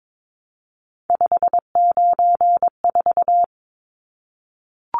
Morse code, heard as a keyed tone, about 22 wpm, 700 Hz. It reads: 594
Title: Numbers - 3-Digit 22wpm